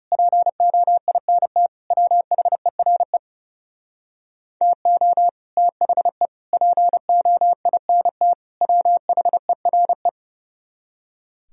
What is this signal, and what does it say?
Morse code, heard as a keyed tone, about 37 wpm, 700 Hz. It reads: POINT WHERE TO THE POINT WHERE
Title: Speed-Racing - Sets of 4 Words 30wpm